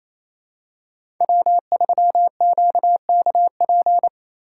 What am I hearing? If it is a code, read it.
Morse code, heard as a keyed tone, about 28 wpm, 700 Hz. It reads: W3QKP
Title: USA Call Signs - 4x Character Spacing - 28wpm